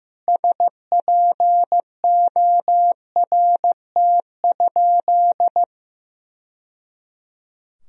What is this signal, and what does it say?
Morse code, heard as a keyed tone, about 15 wpm, 700 Hz. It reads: SPORT?
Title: Sentences from Top 1000 Words - Review Part 4 - 15wpm